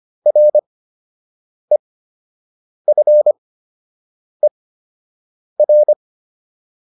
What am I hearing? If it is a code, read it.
Morse code, heard as a keyed tone, about 25 wpm, spaced as for 7 wpm, 600 Hz. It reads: REFER